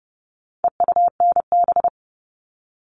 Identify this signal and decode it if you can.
Morse code, heard as a keyed tone, about 30 wpm, 700 Hz. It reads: EUNB
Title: All Cut Numbers - 4-Digit 30wpm